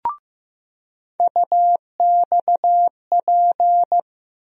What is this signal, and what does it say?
Morse code, heard as a keyed tone, about 15 wpm, 700 Hz. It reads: UXP
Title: USA Call Sign Postfixes - 3 Letters 15wpm